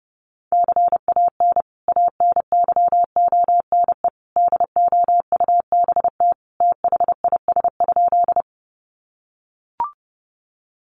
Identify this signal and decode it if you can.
Morse code, heard as a keyed tone, about 30 wpm, 700 Hz. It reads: CAN ANYONE DOUBT THIS?
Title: Sentences from Top 1500 Words 30wpm